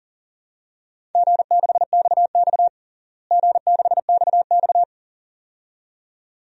Morse code, heard as a keyed tone, about 37 wpm, 700 Hz. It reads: G6XX G6XX
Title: DXCC-100 Call Signs - 4 Characters - Repeated 2x - 40wpm